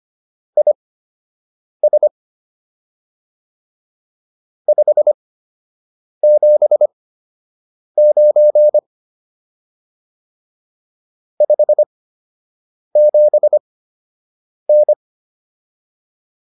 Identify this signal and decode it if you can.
Morse code, heard as a keyed tone, about 25 wpm, spaced as for 7 wpm, 600 Hz. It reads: IS 579 57N